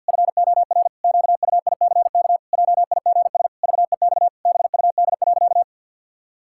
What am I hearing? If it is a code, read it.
Morse code, heard as a keyed tone, about 46 wpm, 700 Hz. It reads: FOR QUICK JIGS VEX BUD.
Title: Pangram Challenge v2 - 50wpm